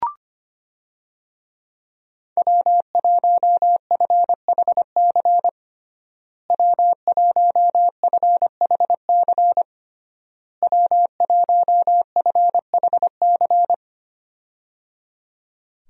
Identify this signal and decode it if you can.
Morse code, heard as a keyed tone, about 25 wpm, 700 Hz. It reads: W1FHC W1FHC W1FHC
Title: USA Call Signs - 5 Characters - Repeated 3x - 25wpm